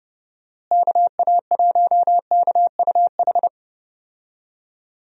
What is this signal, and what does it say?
Morse code, heard as a keyed tone, about 30 wpm, 700 Hz. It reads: KA1KUH
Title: USA Call Signs - 6 Characters - Repeated 3x - 30wpm